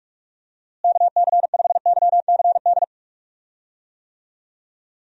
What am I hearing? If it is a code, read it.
Morse code, heard as a keyed tone, about 46 wpm, 700 Hz. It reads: KC5YCD
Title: USA Call Signs - 6 Characters - Repeated 2x - 45wpm